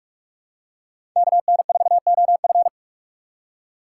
Morse code, heard as a keyed tone, about 41 wpm, 700 Hz. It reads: KN4OF
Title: USA Call Signs - 5 Characters - 4x Character Spacing - 45wpm